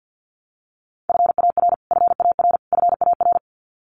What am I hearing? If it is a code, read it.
Morse code, heard as a keyed tone, about 50 wpm, 700 Hz. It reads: FAR FAR FAR
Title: Top 500 Words Repeated 50wpm